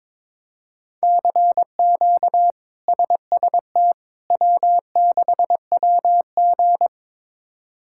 Morse code, heard as a keyed tone, about 21 wpm, 700 Hz. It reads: CQ SST W6WG
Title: K1USN SST QSO Practice - 22wpm